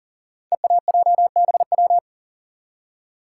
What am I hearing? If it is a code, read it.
Morse code, heard as a keyed tone, about 42 wpm, 690 Hz. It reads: EAJBW
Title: Code Groups - 5 Letters - 40wpm